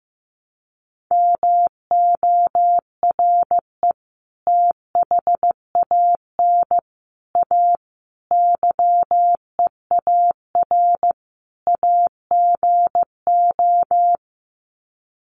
Morse code, heard as a keyed tone, about 15 wpm, 700 Hz. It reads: MORE THAN A YEAR AGO
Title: Sets of 5 Words Encore2 15wpm